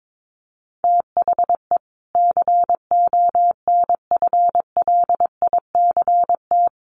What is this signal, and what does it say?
Morse code, heard as a keyed tone, about 22 wpm, 700 Hz. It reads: THE CONFLICT
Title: Sentences from Top 1300 Words 22wpm